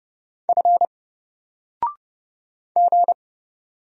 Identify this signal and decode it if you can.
Morse code, heard as a keyed tone, about 30 wpm, 700 Hz. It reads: F G
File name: Single-Letter-Number-30wpm